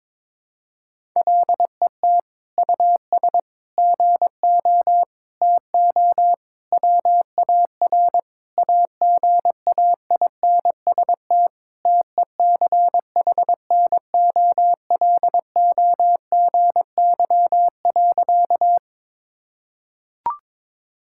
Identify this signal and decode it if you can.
Morse code, heard as a keyed tone, about 22 wpm, 700 Hz. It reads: LET US GO TO WAR AGAINST TECHNOLOGY.
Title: Sentences from Top 1000 Words - Review Part 5 - 22wpm